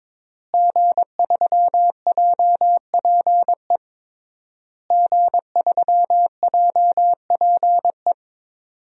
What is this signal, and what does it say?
Morse code, heard as a keyed tone, about 22 wpm, 700 Hz. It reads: G3JPE G3JPE
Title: DXCC-100 Call Signs - 5 Characters - Repeated 3x - 22wpm